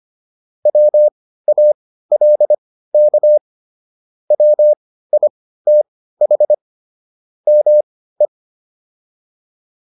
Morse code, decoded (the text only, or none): WALK WITH ME